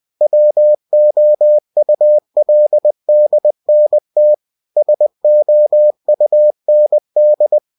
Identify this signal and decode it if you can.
Morse code, heard as a keyed tone, about 20 wpm, 600 Hz. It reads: WOULDNT SOUND
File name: cw-books / winnie-the-pooh / winnie-20wpm-page-0065